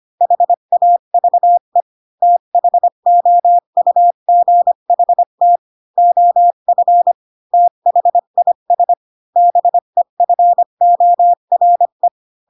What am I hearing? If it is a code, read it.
Morse code, heard as a keyed tone, about 25 wpm, 700 Hz. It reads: HAVE THOUGHT OF THIS BEFORE
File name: Pinocchio / Pinocchio ch3 20 25 ik1hge